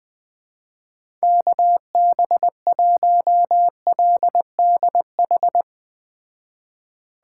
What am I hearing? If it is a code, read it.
Morse code, heard as a keyed tone, about 20 wpm, 700 Hz. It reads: KB1LDH